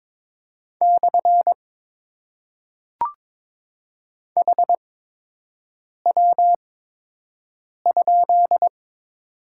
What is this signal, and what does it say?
Morse code, heard as a keyed tone, about 22 wpm, spaced as for 6 wpm, 700 Hz. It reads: / HW?
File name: Top 90 QSO Elements - 8x Character Spacing - 22wpm